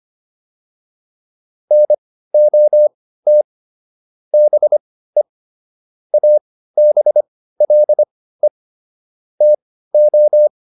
Morse code, heard as a keyed tone, about 25 wpm, spaced as for 15 wpm, 600 Hz. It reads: NOT BE ABLE TO